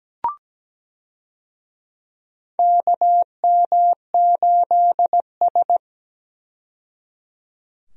Morse code, heard as a keyed tone, about 17 wpm, 700 Hz. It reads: KM8S
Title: USA Call Signs - 4 Characters - 17wpm